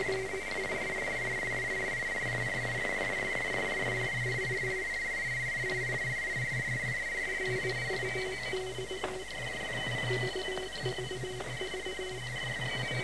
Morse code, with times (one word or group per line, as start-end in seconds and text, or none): none